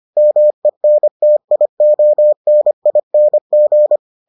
Morse code, heard as a keyed tone, about 25 wpm, 600 Hz. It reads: MENTIONING